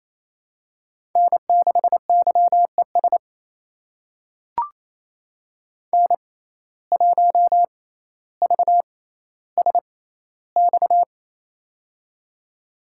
Morse code, heard as a keyed tone, about 28 wpm, spaced as for 14 wpm, 700 Hz. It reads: N6YES N1VSX
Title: USA Call Signs - 5 Characters - 6x Character Spacing - 28wpm